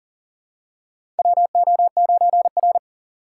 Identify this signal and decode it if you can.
Morse code, heard as a keyed tone, about 40 wpm, 700 Hz. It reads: WO9R
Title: USA Call Signs - 4 Characters - 6x Character Spacing - 40wpm